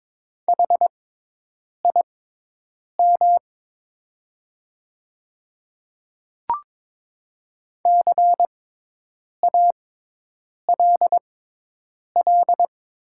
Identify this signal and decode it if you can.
Morse code, heard as a keyed tone, about 22 wpm, spaced as for 8 wpm, 700 Hz. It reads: HIM CALL
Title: 6x Character Spacing - Top 100 Words - 22wpm